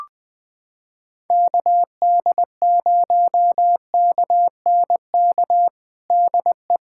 Morse code, heard as a keyed tone, about 20 wpm, 700 Hz. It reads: KD0KNK DE
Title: USA Callsign DE Callsign - 6 Characters - 20wpm